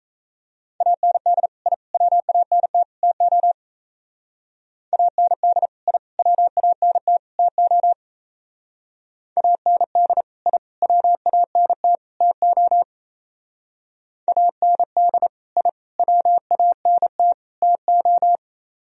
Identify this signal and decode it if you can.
Morse code, heard as a keyed tone, about 34 wpm, 700 Hz. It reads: AND I WANT TO AND I WANT TO AND I WANT TO AND I WANT TO
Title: Speed-Racing - Sets of 4 Words 28wpm